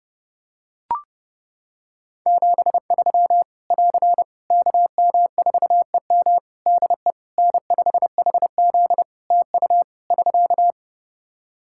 Morse code, heard as a keyed tone, about 30 wpm, 700 Hz. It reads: 73 <AR> KM4EM DE N5HZ TU <SK>
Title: Practice QSO - Exchange 4 - Ending - 30wpm